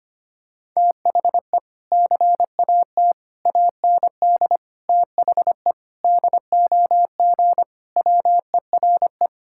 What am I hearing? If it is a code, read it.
Morse code, heard as a keyed tone, about 25 wpm, 700 Hz. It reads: THE CAT AND THE DOG WERE
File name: practice / 25wpm / Sentences with 2-Word Phrases - 25wpm